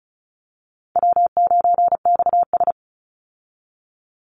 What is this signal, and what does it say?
Morse code, heard as a keyed tone, about 34 wpm, 700 Hz. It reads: W9XS
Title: Call Signs - 35wpm